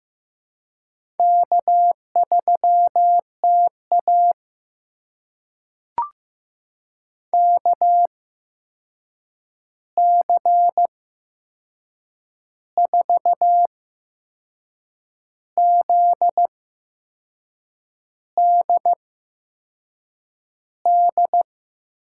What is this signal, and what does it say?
Morse code, heard as a keyed tone, about 15 wpm, spaced as for 6 wpm, 700 Hz. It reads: K3TA KC4ZDD
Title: USA Call Signs - 8x Character Spacing - 15wpm